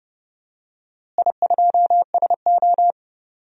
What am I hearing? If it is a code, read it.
Morse code, heard as a keyed tone, about 32 wpm, 700 Hz. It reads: I2SO
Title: DXCC-100 Call Signs - 4 Characters - Repeated 3x - 30wpm